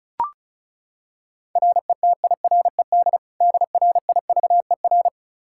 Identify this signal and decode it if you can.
Morse code, heard as a keyed tone, about 36 wpm, 700 Hz. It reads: RETIRED DRIVER
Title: K1USN SST QSO Practice 35wpm